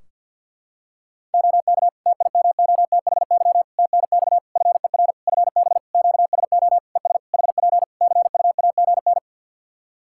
Morse code, heard as a keyed tone, about 51 wpm, 700 Hz. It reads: OK TIMOTHY TNX FER FB QSO ES HP CUAGN
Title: K1USN SST QSO Practice 50wpm